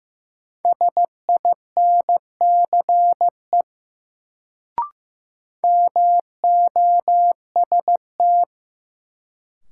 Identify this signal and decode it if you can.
Morse code, heard as a keyed tone, about 15 wpm, 700 Hz. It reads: SINCE MOST
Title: Sets of 2 Words - New Word Drill - 15wpm